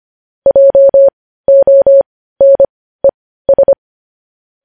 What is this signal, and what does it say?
Morse code, heard as a keyed tone, about 25 wpm, spaced as for 15 wpm, 550 Hz. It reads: JONES
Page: news headlines in morse code at 15 wpm